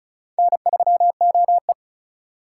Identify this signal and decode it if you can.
Morse code, heard as a keyed tone, about 45 wpm, 700 Hz. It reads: N3OE